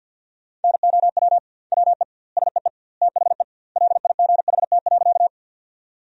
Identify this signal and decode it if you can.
Morse code, heard as a keyed tone, about 50 wpm, 700 Hz. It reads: NOW WE SEE THE LIGHT.
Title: 2x Word-Spacing - Sentences from Top 200 Words - 50wpm